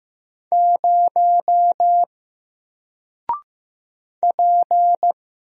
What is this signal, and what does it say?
Morse code, heard as a keyed tone, about 15 wpm, 700 Hz. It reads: 0 P
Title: Single Character (Letter:Number:Punctuation:Prosign) Alt 15wpm